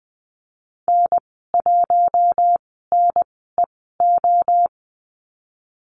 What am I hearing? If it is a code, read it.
Morse code, heard as a keyed tone, about 20 wpm, spaced as for 15 wpm, 700 Hz. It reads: N1NEO
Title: 055 Lesson 10g - Letter r - Call Signs 20wpm